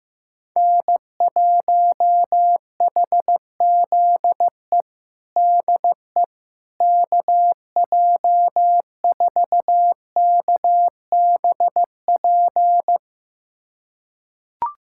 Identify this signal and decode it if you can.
Morse code, heard as a keyed tone, about 16 wpm, 700 Hz. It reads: N1HZE DE KJ4KBP
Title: USA Callsign DE Callsign 15wpm